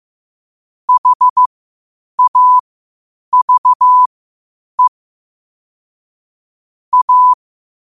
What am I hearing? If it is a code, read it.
Morse code, heard as a keyed tone, about 15 wpm, spaced as for 8 wpm, 1000 Hz. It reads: HAVE A
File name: qotdfiles / 8wpm